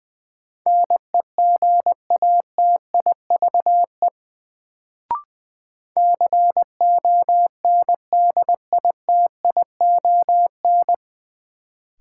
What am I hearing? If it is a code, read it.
Morse code, heard as a keyed tone, about 20 wpm, 700 Hz. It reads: NEGATIVE CONDITION
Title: Sentences from Top 1000 Words - New Word Drill - 20wpm